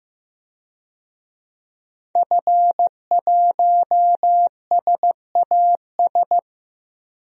Morse code, heard as a keyed tone, about 15 wpm, 700 Hz. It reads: F1SAS
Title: DXCC-100 Call Signs - 5 Characters - Repeated 3x - 15wpm